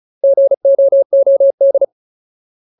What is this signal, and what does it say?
Morse code, heard as a keyed tone, about 34 wpm, 550 Hz. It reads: GOOD
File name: Ultimate Word List / 35WPM Ultimate